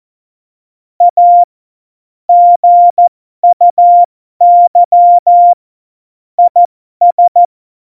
Morse code, heard as a keyed tone, about 14 wpm, spaced as for 12 wpm, 700 Hz. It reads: A GUY IS
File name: Morsekurs / Witze